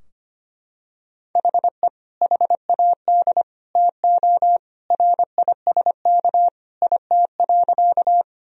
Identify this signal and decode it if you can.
Morse code, heard as a keyed tone, about 25 wpm, 700 Hz. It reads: HE HAD TO RISK IT.
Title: Sentences from Top 1000 Words - Review Part 4 - 25wpm